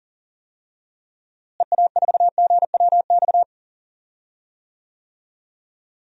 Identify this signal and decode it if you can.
Morse code, heard as a keyed tone, about 40 wpm, 700 Hz. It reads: EA4GWX
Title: DXCC-100 Call Signs - 6 Characters - Repeated 2x - 40wpm